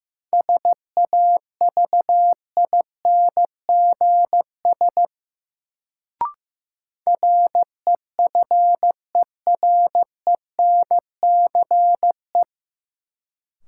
Morse code, heard as a keyed tone, about 15 wpm, 700 Hz. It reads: SAVINGS REFERENCE
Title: Sentences from Top 1200 Words - New Word Drill - 15wpm